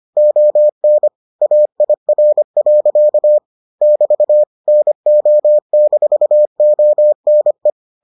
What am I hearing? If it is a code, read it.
Morse code, heard as a keyed tone, about 25 wpm, 600 Hz. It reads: ON AIR. <BT> NO-ONE